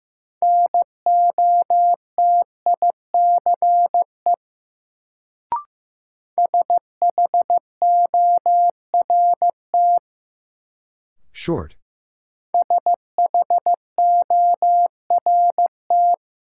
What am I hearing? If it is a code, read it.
Morse code, heard as a keyed tone, about 15 wpm, 700 Hz. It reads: NOTICE SHORT SHORT
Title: Sentences from Top 400 Words - New Word Drill - 15wpm